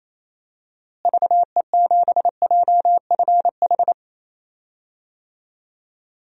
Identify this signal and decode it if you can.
Morse code, heard as a keyed tone, about 28 wpm, 700 Hz. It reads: VE7JFH
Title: DXCC-100 Call Signs - 6 Characters - Repeated 2x - 28wpm